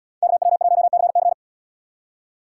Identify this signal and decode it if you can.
Morse code, heard as a keyed tone, about 77 wpm, 700 Hz. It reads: KW1CC